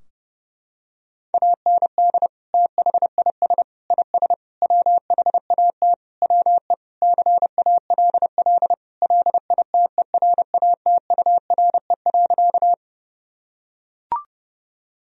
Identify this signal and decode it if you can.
Morse code, heard as a keyed tone, about 30 wpm, 700 Hz. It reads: AND THIS IS WHAT WE CALL LITERATURE.
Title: Sentences from Top 1000 Words - Review Part 3 - 30wpm